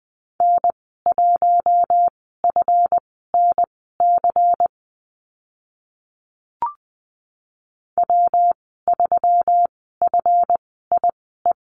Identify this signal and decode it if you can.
Morse code, heard as a keyed tone, about 20 wpm, spaced as for 15 wpm, 700 Hz. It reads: N1FNC W3FIE
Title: 161 Lesson 26g - Letter g - Call Signs 20wpm